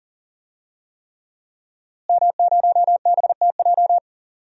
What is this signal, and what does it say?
Morse code, heard as a keyed tone, about 39 wpm, 700 Hz. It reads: M0BTJ
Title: DXCC-100 Call Signs - 5 Characters - Repeated 3x - 40wpm